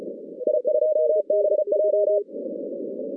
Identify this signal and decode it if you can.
Morse code, heard as a keyed tone, about 34 wpm, 620 Hz. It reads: I2BJ
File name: 2015 cq ww cw / 2015 cqwww k5tr - 0-20